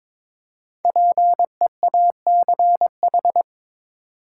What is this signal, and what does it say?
Morse code, heard as a keyed tone, about 22 wpm, 700 Hz. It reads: PEACH
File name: Difficult Words with Hints - 22wpm